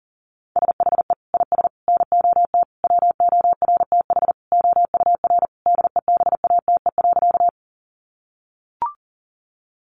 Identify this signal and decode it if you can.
Morse code, heard as a keyed tone, about 40 wpm, 700 Hz. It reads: SHE IS NOT WORTH OUR DEBATE.
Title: Sentences from Top 1100 Words 40wpm